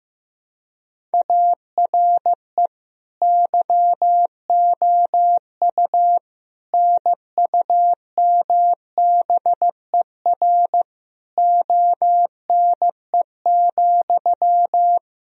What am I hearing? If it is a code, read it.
Morse code, heard as a keyed tone, about 15 wpm, 700 Hz. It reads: ARE YOU NUMBER ONE,
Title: Sentences from Top 1000 Words - Review Part 1 - 15wpm